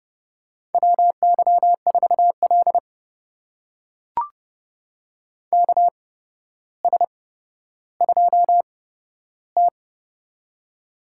Morse code, heard as a keyed tone, about 32 wpm, spaced as for 13 wpm, 700 Hz. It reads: WY4L KS2T
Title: USA Call Signs - 4 Characters - 8x Character Spacing - 30wpm